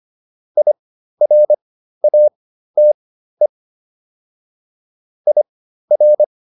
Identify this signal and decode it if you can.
Morse code, heard as a keyed tone, about 25 wpm, spaced as for 13 wpm, 600 Hz. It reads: IRATE IR
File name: IR5 13